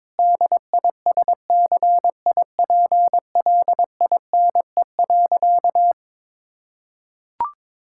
Morse code, heard as a keyed tone, about 22 wpm, 700 Hz. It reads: DISCIPLINE.
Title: Sentences from Top 1000 Words - Review Part 6 - 22wpm